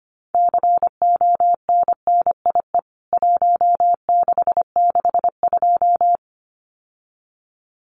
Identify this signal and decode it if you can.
Morse code, heard as a keyed tone, about 25 wpm, 700 Hz. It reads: CONNIE 1662
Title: CWT Exchange 25wpm